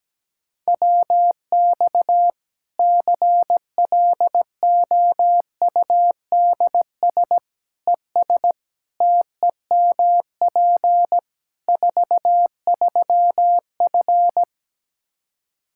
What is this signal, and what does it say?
Morse code, heard as a keyed tone, about 17 wpm, 700 Hz. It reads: WX CLOUDS ES TEMP 43F
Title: K1USN SST QSO Practice 17wpm